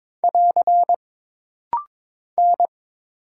Morse code, heard as a keyed tone, about 22 wpm, 700 Hz. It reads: <AR> N
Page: This rapid-fire timing is geared to push your ICR (Instant Character Recognition) skills.